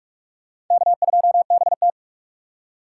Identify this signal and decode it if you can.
Morse code, heard as a keyed tone, about 44 wpm, 700 Hz. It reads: K2BT